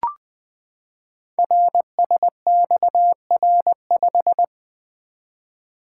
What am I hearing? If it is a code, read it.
Morse code, heard as a keyed tone, about 20 wpm, 700 Hz. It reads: RSXR5
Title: Code Groups - 5 Letter:Numbers - 20wpm